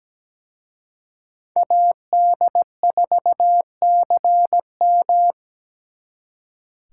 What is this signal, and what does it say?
Morse code, heard as a keyed tone, about 20 wpm, 700 Hz. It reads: AD4CM
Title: Speed Racing - USA 2x2 Call Signs 17wpm